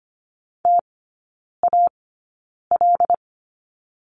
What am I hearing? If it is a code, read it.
Morse code, heard as a keyed tone, about 25 wpm, spaced as for 9 wpm, 700 Hz. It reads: TAL
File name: course / 25:10wpm / 093 Lesson 16e - Letter c - Three Characters 25wpm